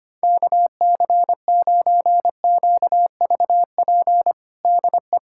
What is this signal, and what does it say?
Morse code, heard as a keyed tone, about 25 wpm, 700 Hz. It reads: KC9QVP DE
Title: USA Callsign DE Callsign - 6 Characters - 25wpm